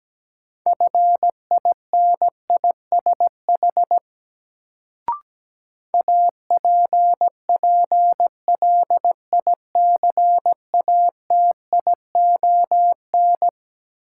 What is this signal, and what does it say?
Morse code, heard as a keyed tone, about 17 wpm, 700 Hz. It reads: FINISH APPLICATION
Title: Sentences from Top 1200 Words - New Word Drill Alt - 17wpm